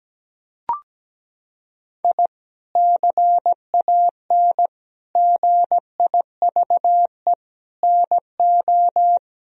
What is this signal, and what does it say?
Morse code, heard as a keyed tone, about 17 wpm, 700 Hz. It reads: I CAN GIVE NO
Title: Sentences from Top 1000 Words - Review Part 3 - 17wpm